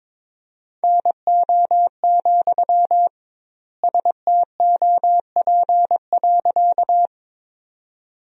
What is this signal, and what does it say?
Morse code, heard as a keyed tone, about 22 wpm, 700 Hz. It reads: NO, STOP.
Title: 2x Word-Spacing - Sentences from Top 300 Words - 22wpm